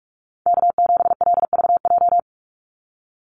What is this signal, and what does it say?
Morse code, heard as a keyed tone, about 44 wpm, 700 Hz. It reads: K7LVJ
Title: Call Signs Repeated 3x - 45wpm